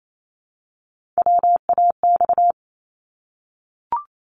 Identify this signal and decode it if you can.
Morse code, heard as a keyed tone, about 28 wpm, 700 Hz. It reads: WAX
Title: Multi-letter Phonemes Repeated 3x 28wpm